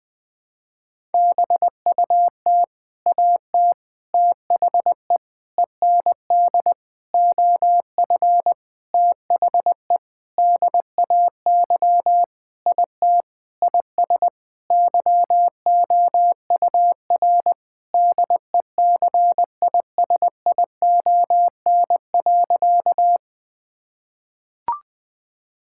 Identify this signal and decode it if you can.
Morse code, heard as a keyed tone, about 20 wpm, 700 Hz. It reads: BUT AT THE END OF THE DAY IT IS YOUR DECISION.
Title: Sentences with 8-Word Phrases - Spoken Beginning - 20wpm